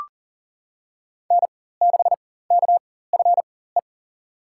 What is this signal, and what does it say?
Morse code, heard as a keyed tone, about 38 wpm, spaced as for 18 wpm, 700 Hz. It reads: N6KFE
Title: USA Call Signs - 5 Characters - 4x Character Spacing - 40wpm